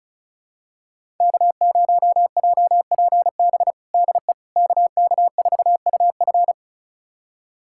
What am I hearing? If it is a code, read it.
Morse code, heard as a keyed tone, about 36 wpm, 700 Hz. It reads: K0JPB DE KK4UF
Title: USA Callsign DE Callsign - 5 Characters - 35wpm